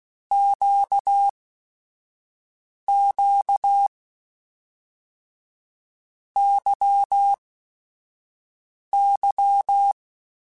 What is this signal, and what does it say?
Morse code, heard as a keyed tone, about 16 wpm, spaced as for 5 wpm, 800 Hz. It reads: QQ YY
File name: Basic Sets / Basic Letter Groups Collective7 / LFCYQ@5